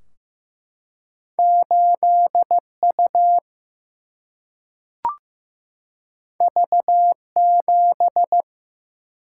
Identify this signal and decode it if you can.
Morse code, heard as a keyed tone, about 15 wpm, 700 Hz. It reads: 8U V7
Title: Single-Letter Single-Number 15wpm